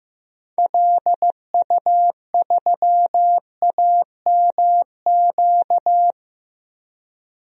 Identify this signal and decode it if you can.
Morse code, heard as a keyed tone, about 15 wpm, 700 Hz. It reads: LU3AMQ
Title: DXCC-100 Call Signs - 6 Characters - 15wpm